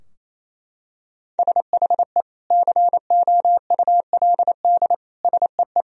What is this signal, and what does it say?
Morse code, heard as a keyed tone, about 27 wpm, 700 Hz. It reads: SHE COULD SEE
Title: Sentences from Top 1000 Words - Review Part 2 - 28wpm